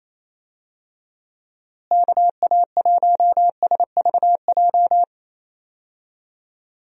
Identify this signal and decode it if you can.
Morse code, heard as a keyed tone, about 28 wpm, 700 Hz. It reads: KA1SVJ
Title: USA Call Signs - 6 Characters - Repeated 2x - 28wpm